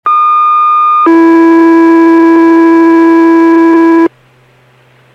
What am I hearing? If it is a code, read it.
Two-tone page, heard as a one-second tone, then a three-second tone, about 1220 then 330 Hz, with noise. It is heard from 0.1 to 4.1 s.